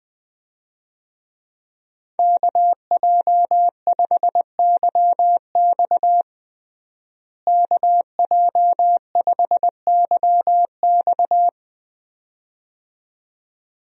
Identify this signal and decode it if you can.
Morse code, heard as a keyed tone, about 20 wpm, 700 Hz. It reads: KJ5YX KJ5YX
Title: USA Call Signs - 5 Characters - Repeated 2x - 20wpm